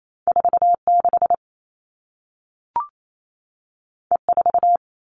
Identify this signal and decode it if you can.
Morse code, heard as a keyed tone, about 28 wpm, 700 Hz. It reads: 46 E4
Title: All Cut Numbers - 2-Digit 28wpm